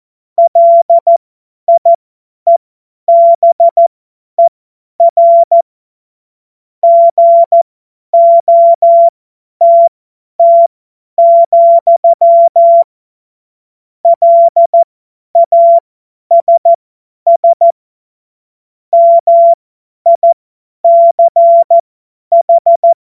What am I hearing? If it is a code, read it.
Morse code, heard as a keyed tone, about 14 wpm, spaced as for 10 wpm, 680 Hz. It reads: LIEBER GOTT, LASS MICH
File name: Morsekurs / Witze / Witz 14. Februar 2023 50 F70